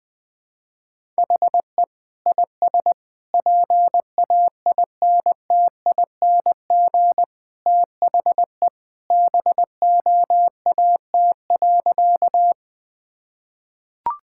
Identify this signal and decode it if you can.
Morse code, heard as a keyed tone, about 20 wpm, 700 Hz. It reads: HE IS PAINTING THE BOAT.
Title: Sentences from Top 1000 Words - Review Part 6 - 20wpm